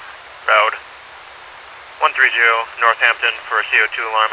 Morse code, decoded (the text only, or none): none